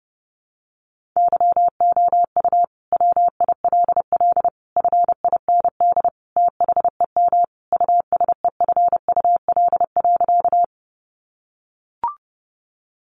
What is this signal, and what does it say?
Morse code, heard as a keyed tone, about 30 wpm, 700 Hz. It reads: YOU WILL FIND THEM USEFUL.
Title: Sentences from Top 700 Words 30wpm